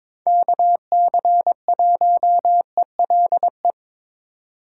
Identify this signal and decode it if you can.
Morse code, heard as a keyed tone, about 22 wpm, 700 Hz. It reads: KC1ELE
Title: USA Call Signs - 6 Characters - 2x Character Spacing - 22wpm